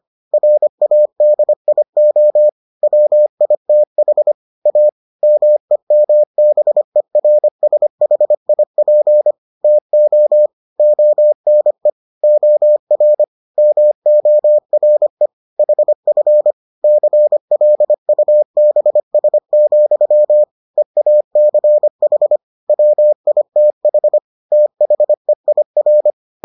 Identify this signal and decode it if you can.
Morse code, heard as a keyed tone, about 25 wpm, 600 Hz. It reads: RADIO WITH A MEMBERSHIP TO ONE OR MORE HF CLUBS, EACH WITH THEIR